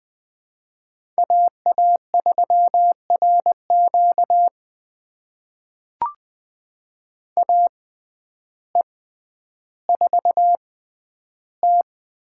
Morse code, heard as a keyed tone, about 20 wpm, spaced as for 11 wpm, 700 Hz. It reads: AA3RQ AE4T
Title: USA Call Signs - 5 Characters - 6x Character Spacing - 20wpm